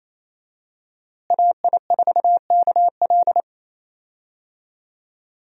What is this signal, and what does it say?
Morse code, heard as a keyed tone, about 27 wpm, 700 Hz. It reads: AI4KL
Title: USA Call Signs - 5 Characters - Repeated 2x - 28wpm